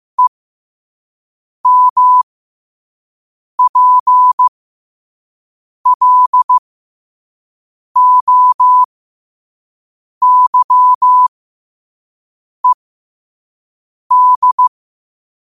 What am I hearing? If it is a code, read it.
Morse code, heard as a keyed tone, about 15 wpm, spaced as for 5 wpm, 1000 Hz. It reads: EMPLOYED
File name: qotdfiles / 5wpm